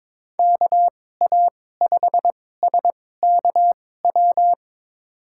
Morse code, heard as a keyed tone, about 21 wpm, spaced as for 16 wpm, 700 Hz. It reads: KA5SKW
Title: USA Call Signs - 6 Characters - 2x Character Spacing - 22wpm